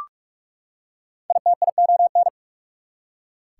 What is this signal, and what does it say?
Morse code, heard as a keyed tone, about 43 wpm, 690 Hz. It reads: ITION
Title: Prefixes and Suffixes 45wpm